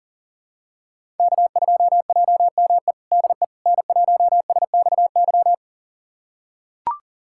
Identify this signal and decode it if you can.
Morse code, heard as a keyed tone, about 37 wpm, 700 Hz. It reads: K2JME DE N1SXY
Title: USA Callsign DE Callsign - 5 Characters - 40wpm